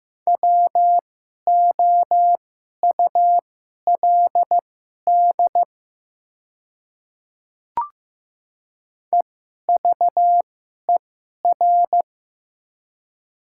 Morse code, heard as a keyed tone, about 15 wpm, spaced as for 11 wpm, 690 Hz. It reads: WOULD EVER
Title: 2x Character Spacing - Top 500 Words - 15wpm